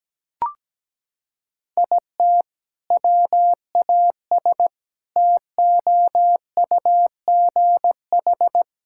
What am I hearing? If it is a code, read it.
Morse code, heard as a keyed tone, about 17 wpm, 700 Hz. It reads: IT WAS TOUGH
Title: Sentences from Top 1600 Words 17wpm